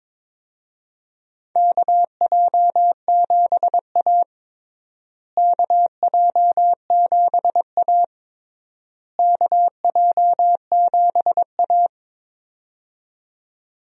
Morse code, heard as a keyed tone, about 22 wpm, 700 Hz. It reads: KJ7A KJ7A KJ7A
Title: USA Call Signs - 4 Characters - Repeated 3x - 22wpm